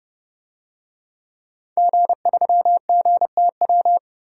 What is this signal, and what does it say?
Morse code, heard as a keyed tone, about 31 wpm, 700 Hz. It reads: G3GTW
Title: DXCC-100 Call Signs - 5 Characters - 30wpm